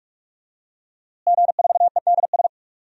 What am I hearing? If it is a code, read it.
Morse code, heard as a keyed tone, about 46 wpm, 700 Hz. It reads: G4EDS
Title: DXCC-100 Call Signs - 5 Characters - Repeated 2x - 45wpm